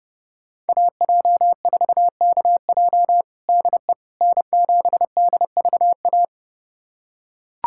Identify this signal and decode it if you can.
Morse code, heard as a keyed tone, about 30 wpm, 700 Hz. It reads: AJ4KJ DE N7DVA
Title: USA Callsign DE Callsign - 5 Characters - 30wpm